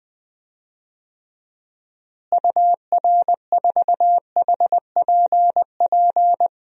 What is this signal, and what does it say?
Morse code, heard as a keyed tone, about 20 wpm, 700 Hz. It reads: UR4HPP
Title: DXCC-100 Call Signs - 6 Characters - Repeated 3x - 20wpm